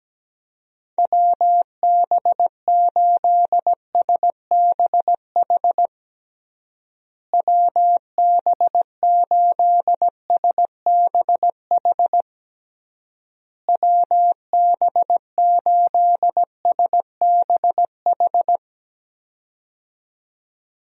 Morse code, heard as a keyed tone, about 17 wpm, 700 Hz. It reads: WB8SBH WB8SBH WB8SBH
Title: USA Call Signs - 6 Characters - Repeated 3x - 17wpm